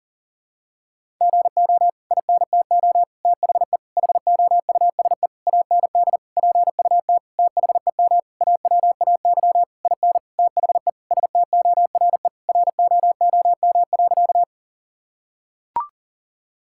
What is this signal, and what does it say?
Morse code, heard as a keyed tone, about 40 wpm, 700 Hz. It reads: GO INTO THE HOUSE AND PUT THEM AWAY IN THE STORE ROOM.
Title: Sentences from Top 1000 Words - Review Part 5 - 40wpm